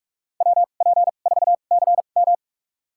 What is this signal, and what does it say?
Morse code, heard as a keyed tone, about 46 wpm, spaced as for 33 wpm, 700 Hz. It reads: WP4CM
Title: USA Call Signs - 5 Characters - 2x Character Spacing - 45wpm